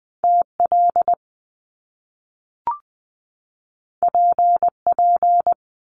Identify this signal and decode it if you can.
Morse code, heard as a keyed tone, about 20 wpm, 700 Hz. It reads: TL PP
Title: Top Two Letter Combinations in Words 20wpm